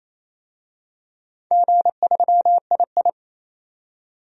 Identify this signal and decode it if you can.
Morse code, heard as a keyed tone, about 28 wpm, 700 Hz. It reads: G3II
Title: DXCC-100 Call Signs - 4 Characters - 28wpm